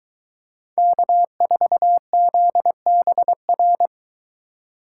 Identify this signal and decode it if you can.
Morse code, heard as a keyed tone, about 22 wpm, 700 Hz. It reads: K4ZBR